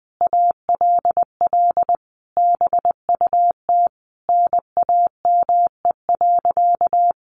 Morse code, heard as a keyed tone, about 20 wpm, 700 Hz. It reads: ALL BUT NAME.
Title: Sentences from Top 1500 Words 20wpm